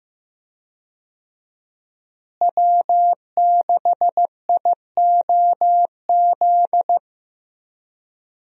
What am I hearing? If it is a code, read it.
Morse code, heard as a keyed tone, about 15 wpm, 700 Hz. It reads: W6IOZ
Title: USA Call Signs - 5 Characters - Repeated 2x - 15wpm